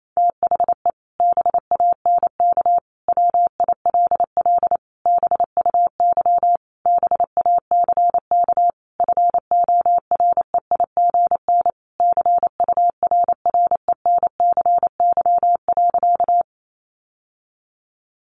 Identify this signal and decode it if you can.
Morse code, heard as a keyed tone, about 28 wpm, 700 Hz. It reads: THE BANK WILL BUY BACK FOREIGN CURRENCY.
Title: Sentences from Top 1500 Words 28wpm